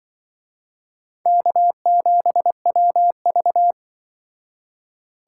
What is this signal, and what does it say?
Morse code, heard as a keyed tone, about 23 wpm, 700 Hz. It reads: K7WV
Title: Speed Racing - USA Call Signs 20wpm